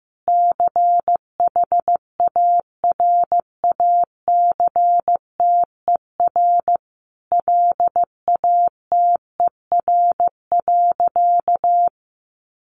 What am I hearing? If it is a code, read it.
Morse code, heard as a keyed tone, about 15 wpm, 700 Hz. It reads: CHARACTER LATER.
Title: Sentences from Top 1200 Words 15wpm